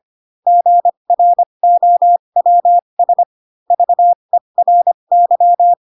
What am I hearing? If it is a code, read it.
Morse code, heard as a keyed tone, about 25 wpm, spaced as for 22 wpm, 700 Hz. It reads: GROWS VERY